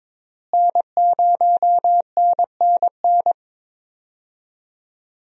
Morse code, heard as a keyed tone, about 22 wpm, 700 Hz. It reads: N0NNN
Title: USA Call Signs - 5 Characters - Repeated 3x - 22wpm